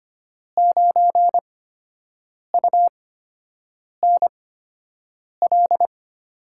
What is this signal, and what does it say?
Morse code, heard as a keyed tone, about 25 wpm, spaced as for 7 wpm, 700 Hz. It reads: 9UNL
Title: USA Call Signs - 6 Characters - 8x Character Spacing - 25wpm